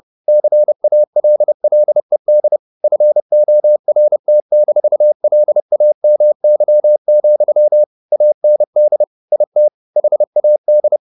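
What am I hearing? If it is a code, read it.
Morse code, heard as a keyed tone, about 30 wpm, 600 Hz. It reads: CALLED FORT-LAMY, AND IT HAD
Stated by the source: Morse code at 30 wpm, the novel Never by Ken Follett in English, from CW med Gnister.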